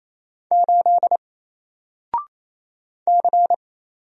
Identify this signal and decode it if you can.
Morse code, heard as a keyed tone, about 27 wpm, 700 Hz. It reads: 8 C
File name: Single-Letter-Number-28wpm